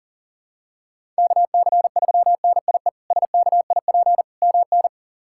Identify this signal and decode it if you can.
Morse code, heard as a keyed tone, about 40 wpm, 700 Hz. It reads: KC3NIE SKIP MN
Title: K1USN SST QSO Practice - 40wpm